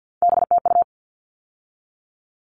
Morse code, heard as a keyed tone, about 50 wpm, 700 Hz. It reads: BTU
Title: All Cut Numbers - 3-Digit 50wpm